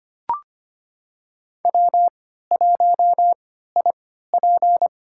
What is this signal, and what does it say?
Morse code, heard as a keyed tone, about 25 wpm, spaced as for 14 wpm, 700 Hz. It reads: W1IP
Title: USA Call Signs - 4 Characters - 3x Character Spacing - 25wpm